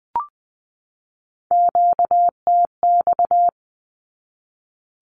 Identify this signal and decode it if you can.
Morse code, heard as a keyed tone, about 20 wpm, 700 Hz. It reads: QTX
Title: Q-Codes 20wpm